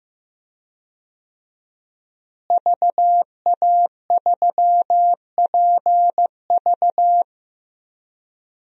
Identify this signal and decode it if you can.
Morse code, heard as a keyed tone, about 15 wpm, 700 Hz. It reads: VA3PV
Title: DXCC-100 Call Signs - 5 Characters - Repeated 3x - 15wpm